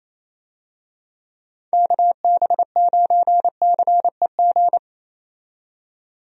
Morse code, heard as a keyed tone, about 27 wpm, 700 Hz. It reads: KB9CEG
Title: USA Call Signs - 6 Characters - 28wpm